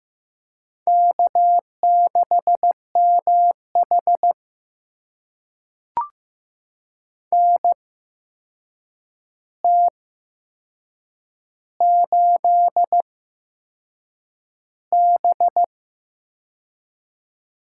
Morse code, heard as a keyed tone, about 15 wpm, spaced as for 6 wpm, 700 Hz. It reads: K6MH NT8B
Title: USA Call Signs - 4 Characters - 8x Character Spacing - 15wpm